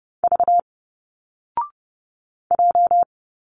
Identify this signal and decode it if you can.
Morse code, heard as a keyed tone, about 30 wpm, 700 Hz. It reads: V J
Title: Single Letters 30wpm